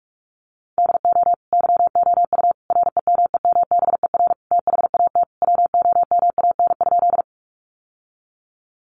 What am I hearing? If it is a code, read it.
Morse code, heard as a keyed tone, about 44 wpm, 700 Hz. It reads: DO YOU REMEMBER THAT WOMAN?
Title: Sentences from Top 1100 Words 45wpm